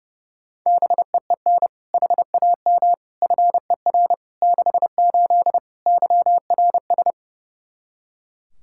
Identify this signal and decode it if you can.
Morse code, heard as a keyed tone, about 32 wpm, 700 Hz. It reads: BEEN HAM FER 68 YRS
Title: Practice QSO - Exchange 3 - More Info - 30wpm